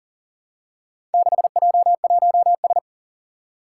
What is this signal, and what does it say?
Morse code, heard as a keyed tone, about 40 wpm, 700 Hz. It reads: 6J1S
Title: Four Challenging Letter-Number Combinations 40wpm